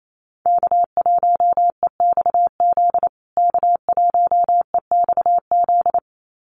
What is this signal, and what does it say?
Morse code, heard as a keyed tone, about 28 wpm, 700 Hz. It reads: K1EXZ K1EXZ
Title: USA Call Signs Repeated 2x 28wpm